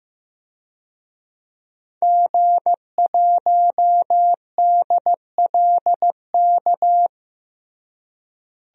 Morse code, heard as a keyed tone, about 15 wpm, 700 Hz. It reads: G1DLK
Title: DXCC-100 Call Signs - 5 Characters - Repeated 2x - 15wpm